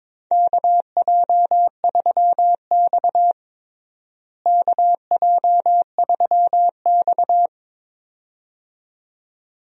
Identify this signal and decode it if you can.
Morse code, heard as a keyed tone, about 22 wpm, 700 Hz. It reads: KJ3X KJ3X
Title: USA Call Signs - 4 Characters - Repeated 2x - 22wpm